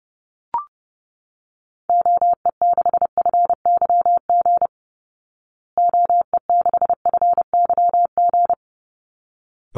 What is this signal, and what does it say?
Morse code, heard as a keyed tone, about 30 wpm, 700 Hz. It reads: OE6FYG OE6FYG
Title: Call Signs Repeated 3x - 30wpm